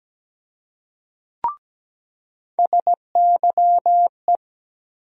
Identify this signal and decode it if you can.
Morse code, heard as a keyed tone, about 17 wpm, 700 Hz. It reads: SYE
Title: USA Call Sign Postfixes - 3 Letters 17wpm